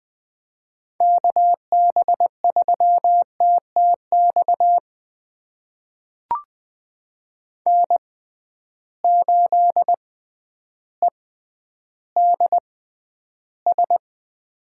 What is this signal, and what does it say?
Morse code, heard as a keyed tone, about 20 wpm, spaced as for 11 wpm, 700 Hz. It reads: KB3TTX N8EDS
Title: USA Call Signs - 6x Character Spacing - 20wpm